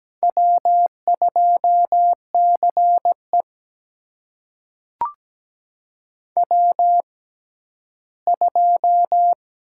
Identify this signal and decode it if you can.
Morse code, heard as a keyed tone, about 17 wpm, spaced as for 12 wpm, 700 Hz. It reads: W2CE W2
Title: USA Call Signs - 4 Characters - 6x Character Spacing - 17wpm